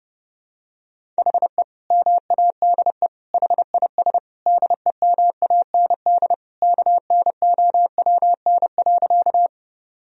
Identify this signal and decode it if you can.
Morse code, heard as a keyed tone, about 32 wpm, 700 Hz. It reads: HE MADE HIS DEMAND KNOWN.
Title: Sentences from Top 1000 Words - Review Part 4 - 30wpm